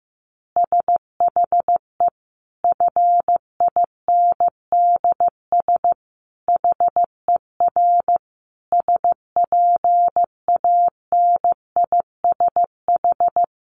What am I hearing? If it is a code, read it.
Morse code, heard as a keyed tone, about 15 wpm, 700 Hz. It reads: SHE FINDS HER SPANISH
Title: Additional Easy Phrases 15wpm